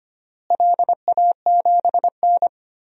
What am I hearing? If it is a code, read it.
Morse code, heard as a keyed tone, about 25 wpm, 700 Hz. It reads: LA7N
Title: DXCC-100 Call Signs - 4 Characters - Repeated 3x - 25wpm